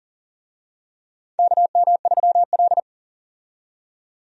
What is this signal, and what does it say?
Morse code, heard as a keyed tone, about 40 wpm, 700 Hz. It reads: KM3L
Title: USA Call Signs - 4 Characters - 40wpm